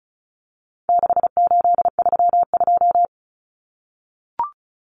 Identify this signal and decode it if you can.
Morse code, heard as a keyed tone, about 34 wpm, 700 Hz. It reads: 6832
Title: Numbers - 4-Digit 35wpm